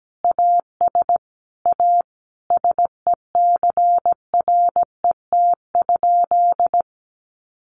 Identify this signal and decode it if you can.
Morse code, heard as a keyed tone, about 17 wpm, 700 Hz. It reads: AS A SECRET?
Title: Sentences from Top 1200 Words 17wpm